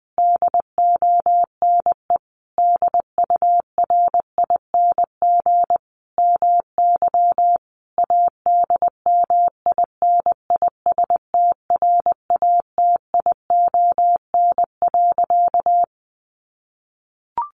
Morse code, heard as a keyed tone, about 20 wpm, 700 Hz. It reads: DONE DURING MY ADMINISTRATION.
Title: Sentences from Top 1300 Words 20wpm